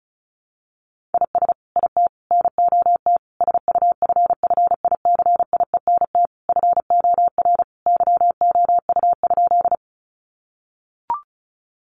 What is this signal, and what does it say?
Morse code, heard as a keyed tone, about 34 wpm, 700 Hz. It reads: IS IT NOT SUFFICIENT FOR YOU?
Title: Sentences from Top 1400 Words 35wpm